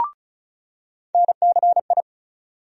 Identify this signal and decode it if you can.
Morse code, heard as a keyed tone, about 34 wpm, 700 Hz. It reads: NCI
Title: USA Call Sign Postfixes - 3 Letters 35wpm